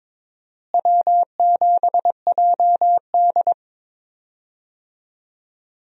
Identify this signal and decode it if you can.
Morse code, heard as a keyed tone, about 22 wpm, 700 Hz. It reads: W7JD
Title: USA Call Signs - 4 Characters - Repeated 3x - 22wpm